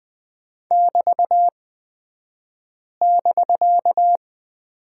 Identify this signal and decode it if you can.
Morse code, heard as a keyed tone, about 20 wpm, spaced as for 5 wpm, 700 Hz. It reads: <BT><BK>
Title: 249 Lesson 40d - BK - Two Characters 20wpm